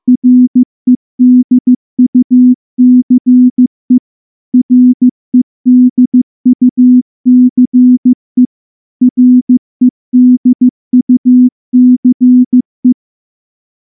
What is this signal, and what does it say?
Morse code, heard as a keyed tone, about 14 wpm, 250 Hz. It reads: REDUCE REDUCE REDUCE
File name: morse-training-15WPM